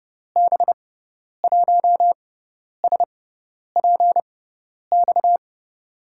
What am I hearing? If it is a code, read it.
Morse code, heard as a keyed tone, about 30 wpm, spaced as for 10 wpm, 700 Hz. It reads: B1SPX
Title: USA Call Signs - 6x Character Spacing - 30wpm